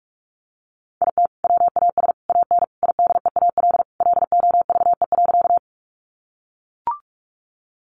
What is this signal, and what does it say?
Morse code, heard as a keyed tone, about 44 wpm, 700 Hz. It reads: IT WAS AN IDEAL LOVE.
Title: Sentences from Top 1400 Words 45wpm